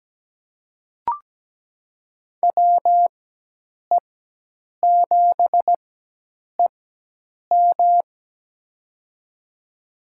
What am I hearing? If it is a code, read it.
Morse code, heard as a keyed tone, about 17 wpm, spaced as for 8 wpm, 700 Hz. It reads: WE7EM
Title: USA Call Signs - 4x Character Spacing - 17wpm